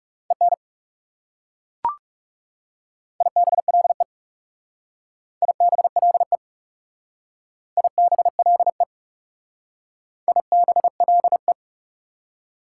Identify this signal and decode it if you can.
Morse code, heard as a keyed tone, about 35 wpm, 690 Hz. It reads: EN IBLE IBLE IBLE IBLE
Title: Speed-Racing Prefixes and Suffixes 30wpm